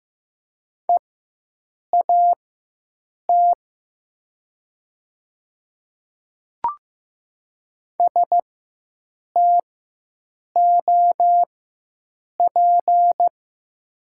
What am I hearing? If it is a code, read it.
Morse code, heard as a keyed tone, about 15 wpm, spaced as for 7 wpm, 700 Hz. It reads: EAT STOP
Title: 4x Character Spacing - Top 500 Words - 15wpm